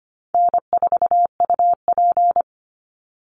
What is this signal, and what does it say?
Morse code, heard as a keyed tone, about 25 wpm, 700 Hz. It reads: N4UP
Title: Easy Call Signs Repeated 2x - 25wpm